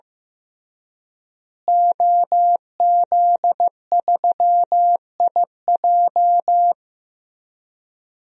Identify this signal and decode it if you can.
Morse code, heard as a keyed tone, about 16 wpm, 700 Hz. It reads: OZ3IJ
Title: DXCC-100 Call Signs - 5 Characters - Repeated 3x - 15wpm